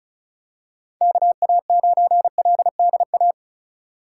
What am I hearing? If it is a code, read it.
Morse code, heard as a keyed tone, about 36 wpm, 700 Hz. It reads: KA9LDA
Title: USA Call Signs - 6 Characters - 2x Character Spacing - 35wpm